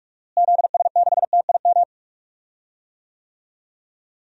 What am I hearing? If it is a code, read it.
Morse code, heard as a keyed tone, about 46 wpm, 700 Hz. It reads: ZS6TIM
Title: DXCC-100 Call Signs - 6 Characters - Repeated 2x - 45wpm